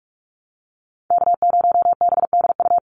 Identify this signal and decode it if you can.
Morse code, heard as a keyed tone, about 44 wpm, 700 Hz. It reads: K0BDU
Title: USA Call Signs 45wpm